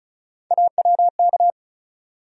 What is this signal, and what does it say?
Morse code, heard as a keyed tone, about 34 wpm, 700 Hz. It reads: AWK
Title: USA Call Sign Postfixes - 3 Letters 35wpm